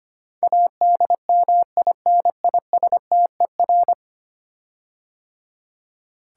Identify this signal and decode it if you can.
Morse code, heard as a keyed tone, about 25 wpm, 700 Hz. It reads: ADMINISTER
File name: 10 Letter Words - First 3 Letters Given - 25wpm